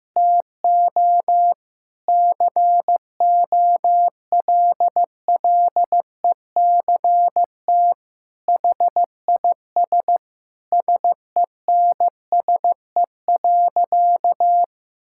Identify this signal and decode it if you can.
Morse code, heard as a keyed tone, about 14 wpm, 700 Hz. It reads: TO COLLECT HIS SENSE.
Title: Sentences from Top 1600 Words 15wpm